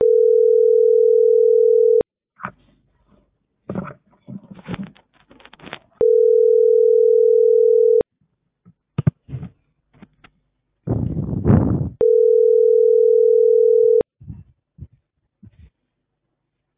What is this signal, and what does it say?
Ringback tone (North America), 440 + 480 Hz.